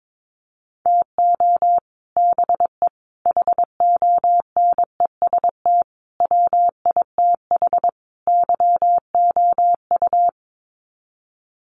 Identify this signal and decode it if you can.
Morse code, heard as a keyed tone, about 22 wpm, 700 Hz. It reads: TO BE HONEST WITH YOU
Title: Sets of 5 Words Encore 22wpm